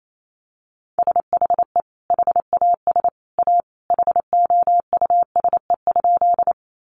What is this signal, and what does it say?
Morse code, heard as a keyed tone, about 28 wpm, 700 Hz. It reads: SHE HAS A HOUSE?
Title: Sentences from Top 200 Words 28wpm